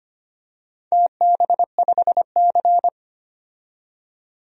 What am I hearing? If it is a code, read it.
Morse code, heard as a keyed tone, about 25 wpm, 700 Hz. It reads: TB5C
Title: DXCC-100 Call Signs - 4 Characters - 25wpm